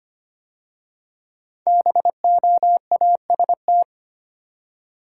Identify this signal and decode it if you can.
Morse code, heard as a keyed tone, about 25 wpm, 700 Hz. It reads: BOAST
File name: 5 Letter Words - First 3 Letters Given - 25wpm